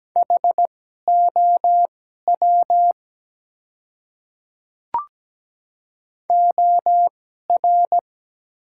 Morse code, heard as a keyed tone, about 17 wpm, spaced as for 12 wpm, 700 Hz. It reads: HOW OR
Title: 2x Character Spacing - Top 100 Words - 17wpm